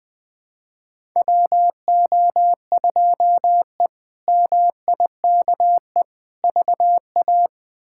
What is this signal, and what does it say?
Morse code, heard as a keyed tone, about 20 wpm, 700 Hz. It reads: WO2E MIKE VA
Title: K1USN SST QSO Practice - 20wpm